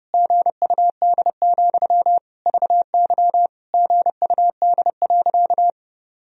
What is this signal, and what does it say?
Morse code, heard as a keyed tone, about 30 wpm, 700 Hz. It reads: GUD, VY GUD.
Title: Sentences from Top 200 Words and Abbreviations 30wpm